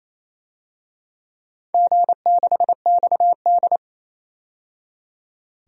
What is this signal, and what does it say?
Morse code, heard as a keyed tone, about 28 wpm, 700 Hz. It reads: G6XD
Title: DXCC-100 Call Signs - 4 Characters - 28wpm